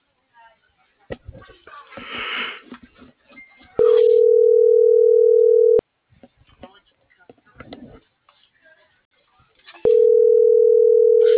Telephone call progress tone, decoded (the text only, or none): ringback tone